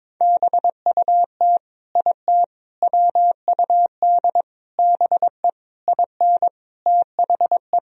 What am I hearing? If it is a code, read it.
Morse code, heard as a keyed tone, about 21 wpm, 700 Hz. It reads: BUT IT WUD BE IN THE
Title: Sentences from Top 300 Words and Abbreviations 22wpm